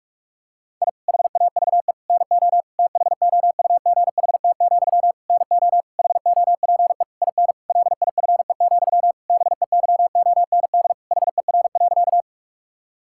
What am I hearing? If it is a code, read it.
Morse code, heard as a keyed tone, about 41 wpm, 700 Hz. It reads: I HAVE NO THOUGHT, NO HOPE IN LIFE, BEYOND HER.
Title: Sentences from Top 1000 Words - Review Part 6 - 45wpm